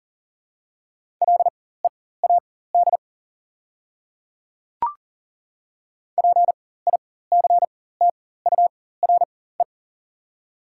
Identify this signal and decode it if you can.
Morse code, heard as a keyed tone, about 40 wpm, spaced as for 19 wpm, 690 Hz. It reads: LEAD PICTURE
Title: 4x Character Spacing - Top 500 Words - 40wpm